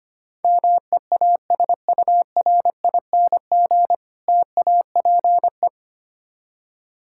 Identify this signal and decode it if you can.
Morse code, heard as a keyed tone, about 25 wpm, 700 Hz. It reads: MEASURING TAPE
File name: practice / 25wpm / Difficult Words with Hints - 25wpm